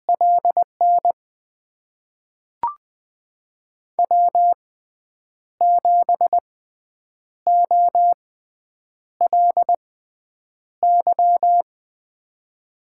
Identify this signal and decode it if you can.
Morse code, heard as a keyed tone, about 20 wpm, spaced as for 8 wpm, 700 Hz. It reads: LN W7OLY
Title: USA Call Signs - 5 Characters - 6x Character Spacing - 20wpm